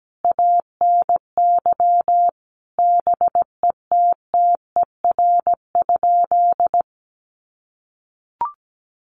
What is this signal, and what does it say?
Morse code, heard as a keyed tone, about 17 wpm, 700 Hz. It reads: ANY BETTER?
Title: Sentences from Top 1200 Words 17wpm